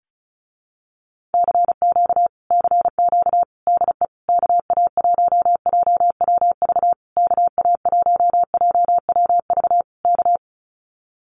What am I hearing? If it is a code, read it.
Morse code, heard as a keyed tone, about 34 wpm, 700 Hz. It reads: CQ CQ DE KA1JWV KA1JWV K
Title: Common QSO Phrases Encore 35wpm